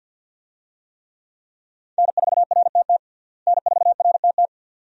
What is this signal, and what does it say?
Morse code, heard as a keyed tone, about 52 wpm, 700 Hz. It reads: N4RTT N4RTT
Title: USA Call Signs - 5 Characters - Repeated 2x - 50wpm